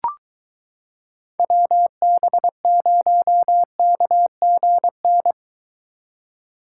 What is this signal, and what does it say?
Morse code, heard as a keyed tone, about 23 wpm, 700 Hz. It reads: WB0KGN